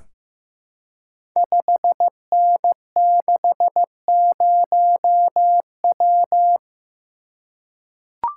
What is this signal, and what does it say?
Morse code, heard as a keyed tone, about 15 wpm, 700 Hz. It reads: 5N60W
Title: Code Groups - 5 Letter:Numbers - 15wpm